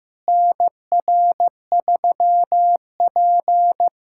Morse code, heard as a keyed tone, about 15 wpm, 700 Hz. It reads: NR3P
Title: USA Callsign DE Callsign - 4 Characters - 15wpm